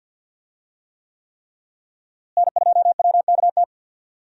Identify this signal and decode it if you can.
Morse code, heard as a keyed tone, about 52 wpm, 700 Hz. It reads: N2WKT
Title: USA Call Signs - 5 Characters - Repeated 3x - 50wpm